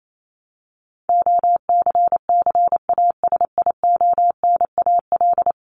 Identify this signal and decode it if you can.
Morse code, heard as a keyed tone, about 28 wpm, 700 Hz. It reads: OCCASIONAL
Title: LY Suffix - Top 100 Words with Base Words 28wpm